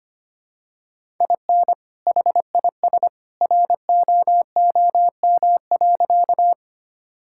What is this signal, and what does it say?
Morse code, heard as a keyed tone, about 25 wpm, 700 Hz. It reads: IN HIS ROOM.
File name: practice / 25wpm / Sentences with 2-Word Phrases - 25wpm